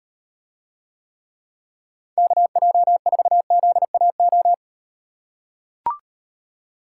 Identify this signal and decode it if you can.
Morse code, heard as a keyed tone, about 38 wpm, 700 Hz. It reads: KJ4ZAO